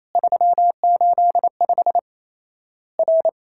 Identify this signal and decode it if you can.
Morse code, heard as a keyed tone, about 28 wpm, 700 Hz. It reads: 385 R
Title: Worked All Europe DX Contest - QTC Exchange - 28wpm